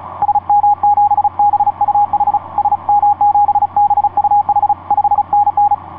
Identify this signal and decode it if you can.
Morse code, heard as a keyed tone, about 36 wpm, 810 Hz, with radio noise. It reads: SM7BUH SM7BUH 5NN
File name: py7rp dl3jxn etc 12m cw 07mar2010 1614